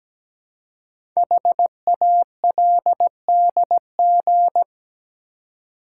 Morse code, heard as a keyed tone, about 17 wpm, 700 Hz. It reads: HALDG
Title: Code Groups - 5 Letters - 17wpm